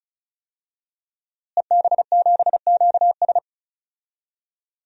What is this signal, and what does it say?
Morse code, heard as a keyed tone, about 36 wpm, 700 Hz. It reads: EB7QS
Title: DXCC-100 Call Signs - 5 Characters - Repeated 3x - 35wpm